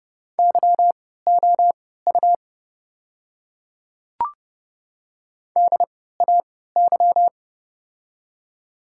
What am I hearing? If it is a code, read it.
Morse code, heard as a keyed tone, about 28 wpm, spaced as for 17 wpm, 690 Hz. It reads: YOU DAY
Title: 3x Character Spacing - Top 100 Words - 30wpm